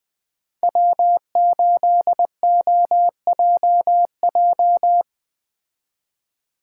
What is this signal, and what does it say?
Morse code, heard as a keyed tone, about 20 wpm, 700 Hz. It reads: W8OJJ
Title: Speed Racing - USA 1x3 Call Signs 17wpm